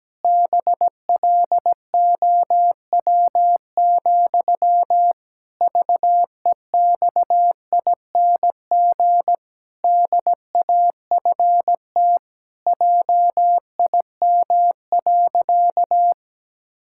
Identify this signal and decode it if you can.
Morse code, heard as a keyed tone, about 17 wpm, 700 Hz. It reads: BLOW, VEXING DAFT JIM.
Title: Pangram Challenge v2 - 17wpm